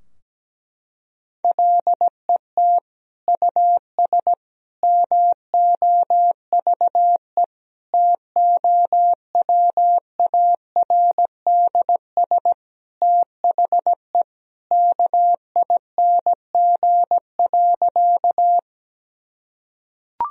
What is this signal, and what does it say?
Morse code, heard as a keyed tone, about 17 wpm, 700 Hz. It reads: LET US MOVE TOWARDS THE KING.
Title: Sentences from Top 1000 Words - Review Part 2 - 17wpm